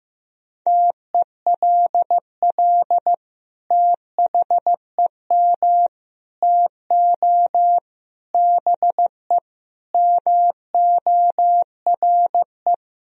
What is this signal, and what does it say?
Morse code, heard as a keyed tone, about 15 wpm, 700 Hz. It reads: TELL THEM TO BE MORE
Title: Sentences from Top 1000 Words - Review Part 3 - 15wpm